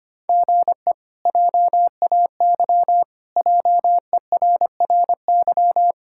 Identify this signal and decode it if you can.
Morse code, heard as a keyed tone, about 25 wpm, 700 Hz. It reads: GE JAY JERRY
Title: K1USN SST QSO Practice - 25wpm